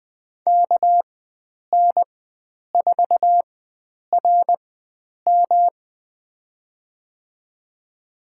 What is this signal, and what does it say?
Morse code, heard as a keyed tone, about 20 wpm, spaced as for 9 wpm, 700 Hz. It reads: KN4RM
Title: USA Call Signs - 5 Characters - 4x Character Spacing - 20wpm